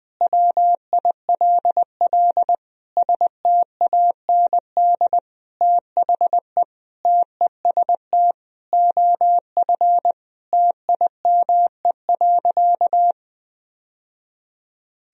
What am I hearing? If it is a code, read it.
Morse code, heard as a keyed tone, about 20 wpm, 700 Hz. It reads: WILL STAND THE TEST OF TIME.
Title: Sentences from Top 1000 Words - Review Part 3 - 20wpm